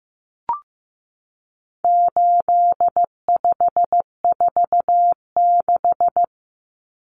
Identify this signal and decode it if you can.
Morse code, heard as a keyed tone, about 15 wpm, 700 Hz. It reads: 8546
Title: Common Cut Numbers - 4-Digit 15wpm